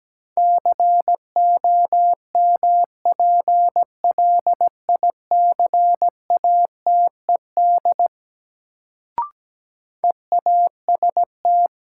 Morse code, heard as a keyed tone, about 17 wpm, 700 Hz. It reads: COMPLICATED EAST
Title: Sentences from Top 1200 Words - New Word Drill Alt - 17wpm